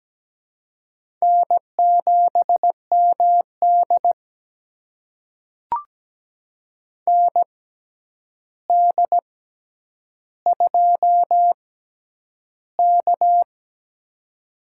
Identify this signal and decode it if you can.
Morse code, heard as a keyed tone, about 17 wpm, spaced as for 9 wpm, 700 Hz. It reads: N7MD ND2K
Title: USA Call Signs - 4 Characters - 6x Character Spacing - 17wpm